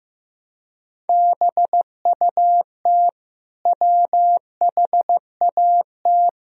Morse code, heard as a keyed tone, about 15 wpm, 700 Hz. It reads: BUT WHAT
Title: Sentences from Top 400 Words and Abbreviations 15wpm